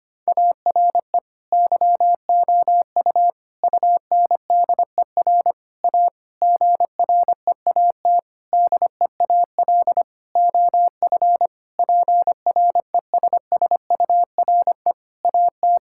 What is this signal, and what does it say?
Morse code, heard as a keyed tone, about 25 wpm, 700 Hz. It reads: ARE YOU UNDER A GREAT DEAL OF PRESSURE AT
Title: Sentences from Top 1000 Words - Review Part 6 - 25wpm